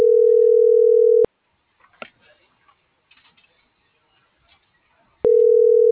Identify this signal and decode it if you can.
Ringback tone (North America), 440 + 480 Hz.
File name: CENTRAL GENERAL / out alexia.jauregui